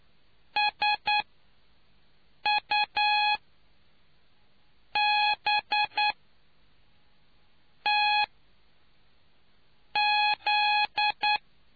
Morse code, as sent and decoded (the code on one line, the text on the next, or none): ... ..- -... - --..
SUBTZ